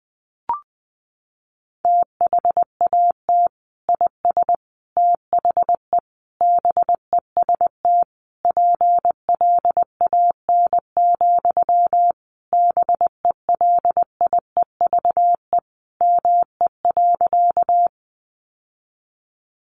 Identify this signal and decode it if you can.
Morse code, heard as a keyed tone, about 20 wpm, 700 Hz. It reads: THAT IS THE BEST PLAN, BELIEVE ME.
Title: Sentences from Top 1000 Words 20wpm